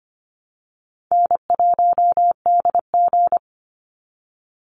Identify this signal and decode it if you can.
Morse code, heard as a keyed tone, about 25 wpm, 700 Hz. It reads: N1DG
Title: Easy Call Signs Repeated 3x - 25wpm